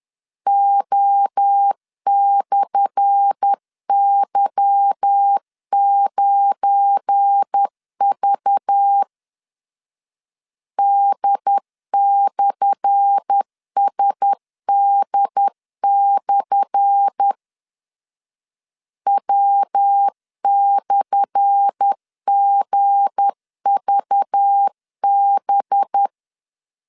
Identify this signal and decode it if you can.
Morse code, heard as a keyed tone, about 10 wpm, 800 Hz. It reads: O/Y9V D/SD/ W/GVB
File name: LEZIONI / 10 SIGNS / 10A SLASH 10wpm 2808